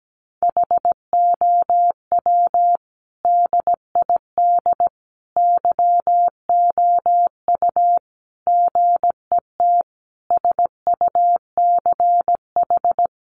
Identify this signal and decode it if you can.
Morse code, heard as a keyed tone, about 17 wpm, 700 Hz. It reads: HOW DID YOU GET SUCH
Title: Sentences from Top 1100 Words 17wpm